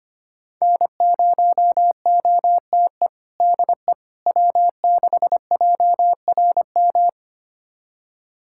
Morse code, heard as a keyed tone, about 25 wpm, 700 Hz. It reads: N0OTE DE W6JRM
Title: USA Callsign DE Callsign - 5 Characters - 25wpm